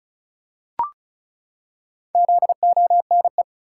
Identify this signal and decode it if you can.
Morse code, heard as a keyed tone, about 36 wpm, 700 Hz. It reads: ZONE
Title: Four Letter Words - 35wpm